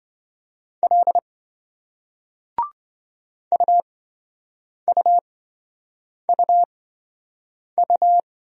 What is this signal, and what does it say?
Morse code, heard as a keyed tone, about 28 wpm, spaced as for 7 wpm, 700 Hz. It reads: L UUUU